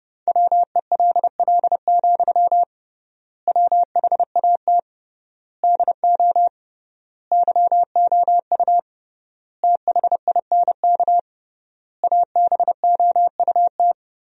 Morse code, heard as a keyed tone, about 29 wpm, 700 Hz. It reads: WELL, WHAT DO YOU THINK ABOUT
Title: 3x Word-Spacing - Sentences from Top 200 Words - 30wpm